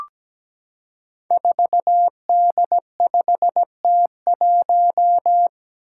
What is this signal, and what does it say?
Morse code, heard as a keyed tone, about 17 wpm, 700 Hz. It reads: 4D5T1
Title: Five Challenging Letter-Number Combinations 17wpm